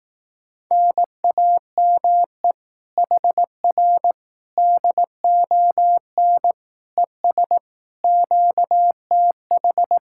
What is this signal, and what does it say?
Morse code, heard as a keyed tone, about 18 wpm, 700 Hz. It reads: NAME HR DON ES QTH
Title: Speed-Racing Common QSO Phrases Encore - 15wpm